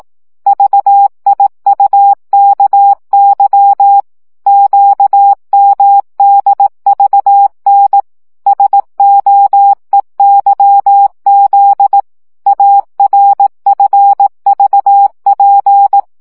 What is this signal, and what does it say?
Morse code, heard as a keyed tone, about 18 wpm, 800 Hz. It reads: VIUKY QMDVN SOEYZ ARFVP
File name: morze vizsga